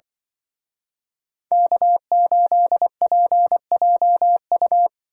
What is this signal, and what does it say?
Morse code, heard as a keyed tone, about 24 wpm, 700 Hz. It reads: K8PJU
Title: Speed Racing - USA 1x3 Call Signs 20wpm